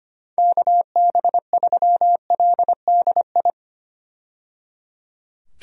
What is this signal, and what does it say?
Morse code, heard as a keyed tone, about 25 wpm, 700 Hz. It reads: KB3LDI
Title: USA Call Signs - 6 Characters - 25wpm